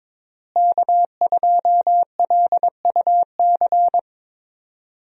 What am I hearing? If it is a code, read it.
Morse code, heard as a keyed tone, about 21 wpm, 700 Hz. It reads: K2LUC
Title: USA Call Signs - 5 Characters - Repeated 3x - 22wpm